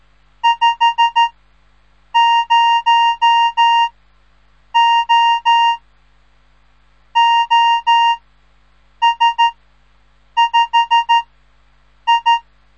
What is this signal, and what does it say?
Morse code, heard as a keyed tone, about 13 wpm, 930 Hz. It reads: 5 0 O O S 5 I
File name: radiomorningstar / morse / les 3